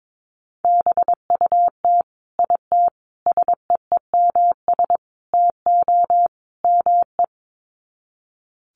Morse code, heard as a keyed tone, about 22 wpm, 700 Hz. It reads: BUT IT SEEMS TO ME
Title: Sets of 5 Words Encore2 22wpm